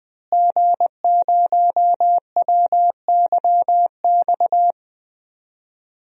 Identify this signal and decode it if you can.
Morse code, heard as a keyed tone, about 20 wpm, 700 Hz. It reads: G0WYX
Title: DXCC-100 Call Signs - 5 Characters - Repeated 2x - 20wpm